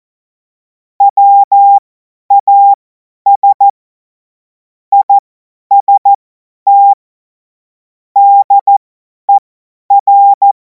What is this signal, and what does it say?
Morse code, heard as a keyed tone, about 14 wpm, spaced as for 10 wpm, 800 Hz. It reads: WAS IST DER
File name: Morsekurs / Witze / Witz 07. Juli 2026 50 F70